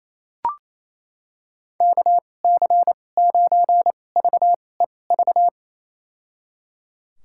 Morse code, heard as a keyed tone, about 28 wpm, spaced as for 20 wpm, 700 Hz. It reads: KC9VEV
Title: USA Call Signs - 6 Characters - 2x Character Spacing - 28wpm